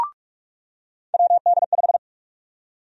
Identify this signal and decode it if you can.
Morse code, heard as a keyed tone, about 43 wpm, 700 Hz. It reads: WD5
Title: USA Call Sign Prefixes - 45wpm